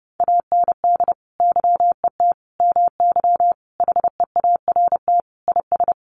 Morse code, heard as a keyed tone, about 30 wpm, 700 Hz. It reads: AND YET MY HEART IS
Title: Sentences from Top 1200 Words 30wpm